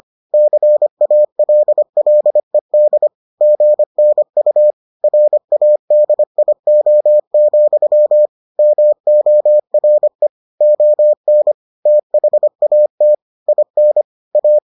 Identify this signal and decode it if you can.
Morse code, heard as a keyed tone, about 25 wpm, 600 Hz. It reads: CALLED GNU RADIO, MORE ON THAT IN A